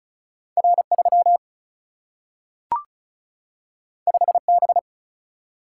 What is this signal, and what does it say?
Morse code, heard as a keyed tone, about 36 wpm, 700 Hz. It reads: R3 5B
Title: Single-Letter Single-Number 35wpm